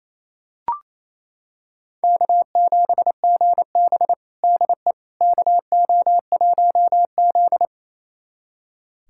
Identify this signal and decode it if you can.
Morse code, heard as a keyed tone, about 28 wpm, 700 Hz. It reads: K7GB DE KO1Z
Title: USA Callsign DE Callsign - 4 Characters - 28wpm